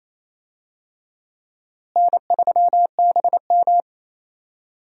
Morse code, heard as a keyed tone, about 28 wpm, 700 Hz. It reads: N3BM
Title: USA Call Signs - 4 Characters - Repeated 2x - 28wpm